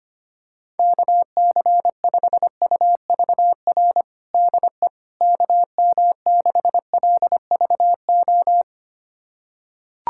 Morse code, heard as a keyed tone, about 25 wpm, 700 Hz. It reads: KC5UVR DE KM6LVO
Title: USA Callsign DE Callsign 25wpm